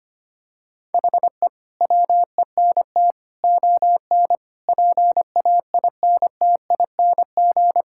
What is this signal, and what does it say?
Morse code, heard as a keyed tone, about 25 wpm, 700 Hz. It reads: HE WENT ON PAINTING
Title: Sentences from Top 1000 Words - Review Part 6 - 25wpm